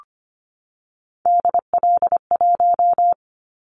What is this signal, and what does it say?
Morse code, heard as a keyed tone, about 25 wpm, 700 Hz. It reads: DL1
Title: Callsign Prefixes for Top 25 DXCC - 25wpm